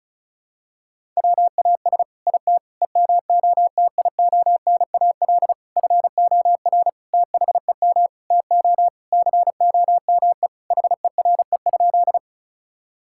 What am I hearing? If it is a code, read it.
Morse code, heard as a keyed tone, about 34 wpm, 690 Hz. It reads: WAS IT EMOTIONAL FOR THEM TO COME HERE?
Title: Sentences from Top 1000 Words - Review Part 5 - 35wpm